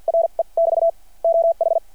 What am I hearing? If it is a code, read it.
Morse code, heard as a keyed tone, about 49 wpm, spaced as for 42 wpm, 660 Hz. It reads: RE <BT> OH
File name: archived / CRSnet